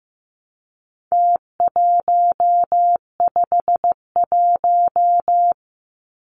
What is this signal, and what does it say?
Morse code, heard as a keyed tone, about 15 wpm, 700 Hz. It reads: T151
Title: Numbers - Contest Style - 4-Digit 15wpm